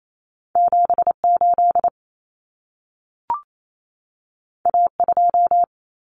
Morse code, heard as a keyed tone, about 28 wpm, 700 Hz. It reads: 78 A2
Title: Common Cut Numbers - 2-Digit 28wpm